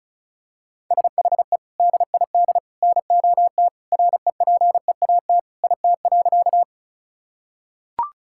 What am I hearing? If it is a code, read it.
Morse code, heard as a keyed tone, about 34 wpm, 700 Hz. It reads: SHE DID NOT REPEAT IT.
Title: Sentences from Top 1600 Words 35wpm